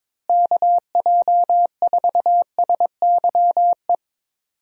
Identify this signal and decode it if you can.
Morse code, heard as a keyed tone, about 22 wpm, 700 Hz. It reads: KJ4SYE
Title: USA Call Signs - 6 Characters - Repeated 2x - 22wpm